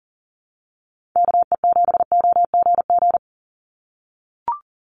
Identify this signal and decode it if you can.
Morse code, heard as a keyed tone, about 40 wpm, 700 Hz. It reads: KE7OGG
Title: USA Call Signs 40wpm